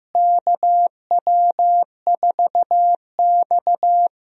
Morse code, heard as a keyed tone, about 15 wpm, 700 Hz. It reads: KW4X
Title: USA Call Signs - 4 Characters - Repeated 2x - 15wpm